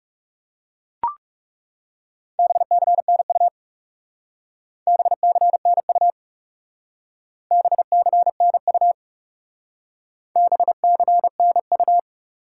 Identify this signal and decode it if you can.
Morse code, heard as a keyed tone, about 36 wpm, 700 Hz. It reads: BCNU BCNU BCNU BCNU
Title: Speed-Racing Top 180 QSO Elements 30wpm